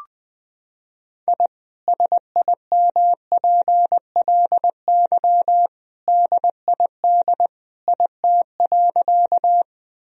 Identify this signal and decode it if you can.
Morse code, heard as a keyed tone, about 20 wpm, 700 Hz. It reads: I SIMPLY DID IT.
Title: Sentences from Top 1000 Words - Review Part 4 - 20wpm